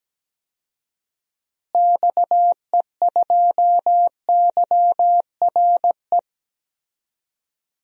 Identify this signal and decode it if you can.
Morse code, heard as a keyed tone, about 17 wpm, 700 Hz. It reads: XE2YRE
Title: DXCC-100 Call Signs - 6 Characters - Repeated 2x - 17wpm